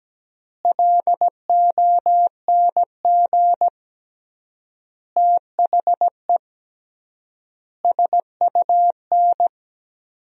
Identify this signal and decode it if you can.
Morse code, heard as a keyed tone, about 17 wpm, 700 Hz. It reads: LONG THE SUN
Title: 3x Word-Spacing - Sentences from Top 400 Words - 17wpm